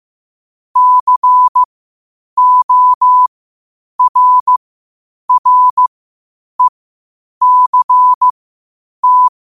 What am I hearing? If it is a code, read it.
Morse code, heard as a keyed tone, about 15 wpm, spaced as for 8 wpm, 1000 Hz. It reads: CORRECT